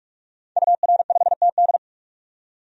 Morse code, heard as a keyed tone, about 46 wpm, 700 Hz. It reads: UR5TD
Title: DXCC-100 Call Signs - 5 Characters - 45wpm